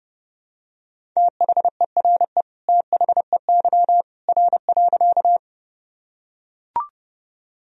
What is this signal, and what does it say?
Morse code, heard as a keyed tone, about 30 wpm, 700 Hz. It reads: THERE THEY R.
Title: Sentences from Top 100 Words and Abbreviations 30wpm